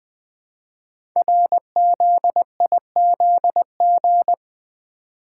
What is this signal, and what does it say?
Morse code, heard as a keyed tone, about 20 wpm, 700 Hz. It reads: RZIZG
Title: Code Groups - 5 Letters - 20wpm